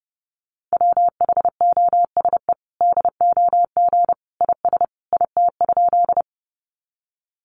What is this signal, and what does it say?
Morse code, heard as a keyed tone, about 30 wpm, 700 Hz. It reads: WHOSE DOG IS IT?
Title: Sentences from Top 1200 Words 30wpm